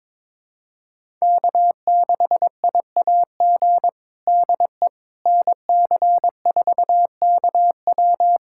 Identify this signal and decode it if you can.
Morse code, heard as a keyed tone, about 22 wpm, 700 Hz. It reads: K6IAG DE NC4KW
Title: USA Callsign DE Callsign - 5 Characters - 22wpm